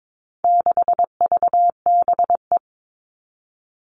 Morse code, heard as a keyed tone, about 22 wpm, 700 Hz. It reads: 6VBE
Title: All Cut Numbers - 4-Digit 22wpm